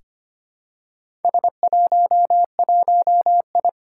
Morse code, heard as a keyed tone, about 25 wpm, 700 Hz. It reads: S11I
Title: Four Challenging Letter-Number Combinations 25wpm